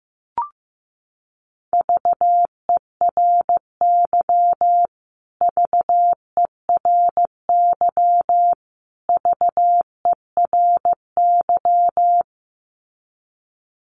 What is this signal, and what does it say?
Morse code, heard as a keyed tone, about 15 wpm, 700 Hz. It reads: VERY VERY VERY
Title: Top 300 Words Repeated 15wpm